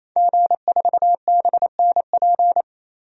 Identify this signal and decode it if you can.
Morse code, heard as a keyed tone, about 28 wpm, 700 Hz. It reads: G4BNP
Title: DXCC-100 Call Signs - 5 Characters - Repeated 3x - 28wpm